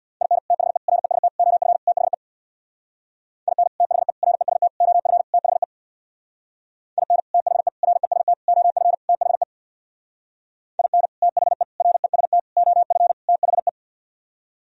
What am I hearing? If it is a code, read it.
Morse code, heard as a keyed tone, about 67 wpm, 690 Hz. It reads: IN THE REST OF THE IN THE REST OF THE IN THE REST OF THE IN THE REST OF THE